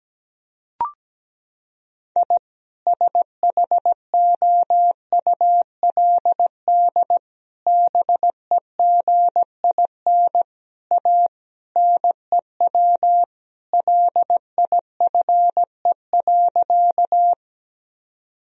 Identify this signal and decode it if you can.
Morse code, heard as a keyed tone, about 17 wpm, 700 Hz. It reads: I SHOULD BEGIN A NEW LIFE.
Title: Sentences from Top 1000 Words - Review Part 1 - 17wpm